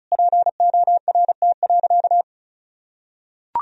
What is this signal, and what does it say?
Morse code, heard as a keyed tone, about 36 wpm, 700 Hz. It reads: PORT.
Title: Sentences from Top 1000 Words - Review Part 2 - 35wpm